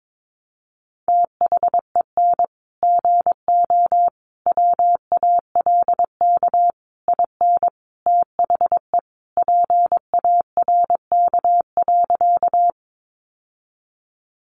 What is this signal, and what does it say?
Morse code, heard as a keyed tone, about 22 wpm, 700 Hz. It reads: THEN GO WALK IN THE PARK.
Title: Sentences from Top 1300 Words 22wpm